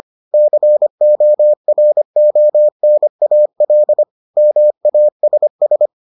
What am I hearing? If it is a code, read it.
Morse code, heard as a keyed tone, about 25 wpm, 600 Hz. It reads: CORONAL MASS